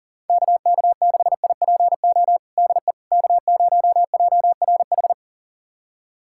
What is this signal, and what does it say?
Morse code, heard as a keyed tone, about 37 wpm, 700 Hz. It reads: KK6IPO DE K0JRH
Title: USA Callsign DE Callsign 40wpm